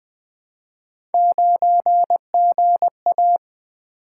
Morse code, heard as a keyed tone, about 20 wpm, 700 Hz. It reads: 9GA
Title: Three Letter-Number Combinations - 20wpm